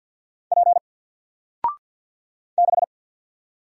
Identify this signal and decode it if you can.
Morse code, heard as a keyed tone, about 48 wpm, 700 Hz. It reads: P 6
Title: Single Letter-Number Alt 50wpm